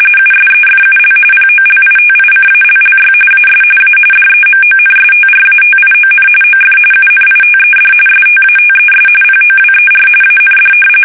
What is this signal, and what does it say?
RTTY, about 45.45 baud. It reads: USEFUL POLITICALLY AS A DEMONSTRATION OF UNITED STATES POWER AND
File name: fsk850 / usnavy3